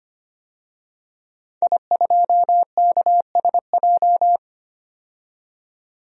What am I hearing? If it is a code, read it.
Morse code, heard as a keyed tone, about 25 wpm, 700 Hz. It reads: I2KSJ
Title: DXCC-100 Call Signs - 5 Characters - 25wpm